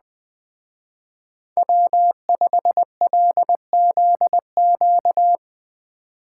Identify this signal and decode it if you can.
Morse code, heard as a keyed tone, about 20 wpm, 700 Hz. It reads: W5LZQ
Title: Speed Racing - USA 1x3 Call Signs 17wpm